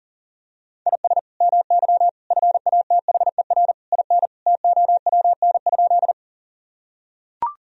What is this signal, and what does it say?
Morse code, heard as a keyed tone, about 37 wpm, 700 Hz. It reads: IS MY FATHER IN TOWN?
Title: Sentences from Top 1000 Words - Review Part 2 - 40wpm